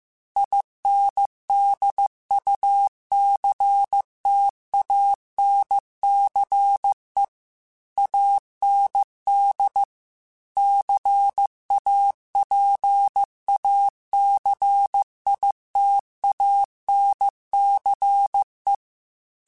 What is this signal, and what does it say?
Morse code, heard as a keyed tone, about 15 wpm, 800 Hz. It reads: INDUCTANCE AND CAPACITANCE
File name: Technical Subjects / Technical Subjects-mp3s / Reactance @13